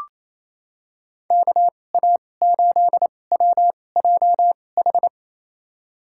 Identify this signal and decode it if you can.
Morse code, heard as a keyed tone, about 28 wpm, spaced as for 20 wpm, 700 Hz. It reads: KA8WJH
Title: USA Call Signs - 6 Characters - 2x Character Spacing - 28wpm